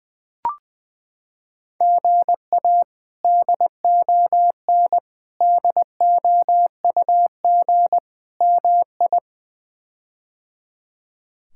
Morse code, heard as a keyed tone, about 20 wpm, 700 Hz. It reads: GA DON DOUG MI
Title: K1USN SST QSO Practice - 20wpm